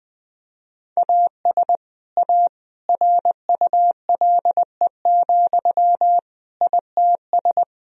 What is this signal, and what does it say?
Morse code, heard as a keyed tone, about 20 wpm, 700 Hz. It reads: AS A RULE, ITS
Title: Sentences from Top 1000 Words - Review Part 2 - 20wpm